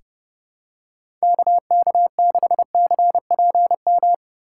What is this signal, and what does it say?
Morse code, heard as a keyed tone, about 29 wpm, 700 Hz. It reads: KK6CPM
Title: USA Call Signs - 6 Characters - 3x Character Spacing - 30wpm